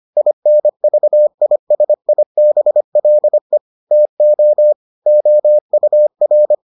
Morse code, heard as a keyed tone, about 25 wpm, 600 Hz. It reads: INVISIBLE TO OUR